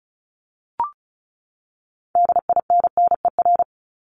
Morse code, heard as a keyed tone, about 34 wpm, 700 Hz. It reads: DINNER
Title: Top 1751-2000 Words 35wpm